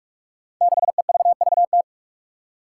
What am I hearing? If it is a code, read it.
Morse code, heard as a keyed tone, about 44 wpm, 700 Hz. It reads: 6EVVT
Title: Five Challenging Letter-Number Combinations 45wpm